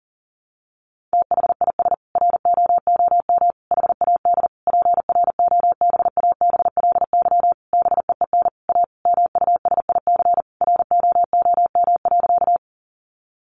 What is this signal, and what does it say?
Morse code, heard as a keyed tone, about 40 wpm, 700 Hz. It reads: THIS ROOM HAD PROBABLY BEEN A MUSIC ROOM.
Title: Sentences from Top 700 Words 40wpm